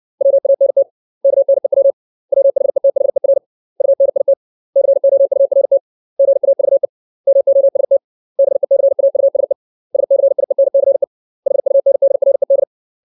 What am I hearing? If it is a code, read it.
Morse code, heard as a keyed tone, about 60 wpm, 550 Hz. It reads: WANT KNEW WHETHER UNIT COURT GAVE MOST BECAUSE SCIENCE HUNDRED
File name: 632 words / 632 Words 60WPM